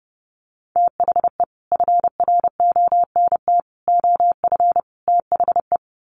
Morse code, heard as a keyed tone, about 30 wpm, 700 Hz. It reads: THE FRONT OF THE
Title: Sets of 4 Words 30wpm